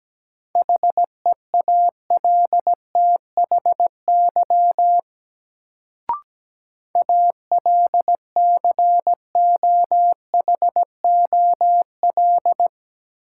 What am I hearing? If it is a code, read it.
Morse code, heard as a keyed tone, about 17 wpm, 700 Hz. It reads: HEALTHY ALCOHOL
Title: Sentences from Top 1400 Words - New Word Drill Alt - 17wpm